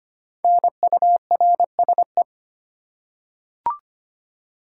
Morse code, heard as a keyed tone, about 25 wpm, 700 Hz. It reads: NURSE
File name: 5 Letter Words - First 3 Letters Given - 25wpm